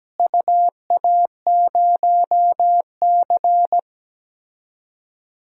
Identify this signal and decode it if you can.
Morse code, heard as a keyed tone, about 17 wpm, 700 Hz. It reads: UA0C
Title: DXCC-100 Call Signs - 4 Characters - Repeated 2x - 17wpm